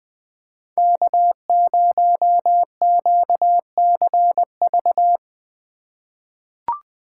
Morse code, heard as a keyed tone, about 20 wpm, 700 Hz. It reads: K0QCV
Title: USA Call Signs - 5 Characters - 3x Character Spacing - 20wpm